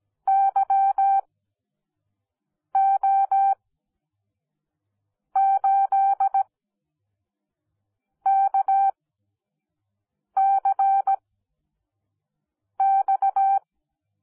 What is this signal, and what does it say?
Morse code, heard as a keyed tone, about 17 wpm, spaced as for 5 wpm, 780 Hz. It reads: YO8KCX
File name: Onlinelehrgang / morsen / DJ4UF Morsen10